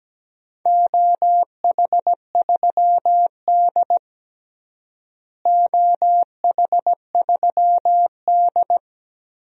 Morse code, heard as a keyed tone, about 17 wpm, 700 Hz. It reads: OH3D OH3D
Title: DXCC-100 Call Signs - 4 Characters - Repeated 2x - 17wpm